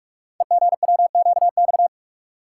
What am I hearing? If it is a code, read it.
Morse code, heard as a keyed tone, about 44 wpm, 690 Hz. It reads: EGWQX
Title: Code Groups - 5 Letters - 45wpm